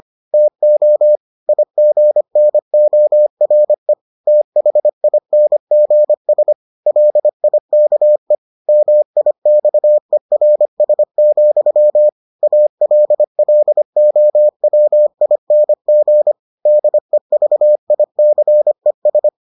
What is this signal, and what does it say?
Morse code, heard as a keyed tone, about 25 wpm, 600 Hz. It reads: TO IGNORE THINGS LIKE MIXERS, ALLOWING DEVICES